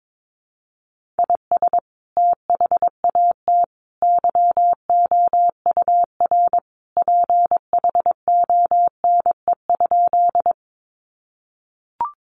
Morse code, heard as a keyed tone, about 22 wpm, 700 Hz. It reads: IS THAT YOUR PHONE?
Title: Sentences from Top 1100 Words 22wpm